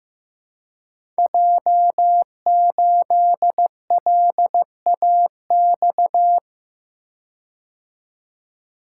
Morse code, heard as a keyed tone, about 15 wpm, 700 Hz. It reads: J8LAX
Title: Code Groups - 5 Letter:Numbers - 15wpm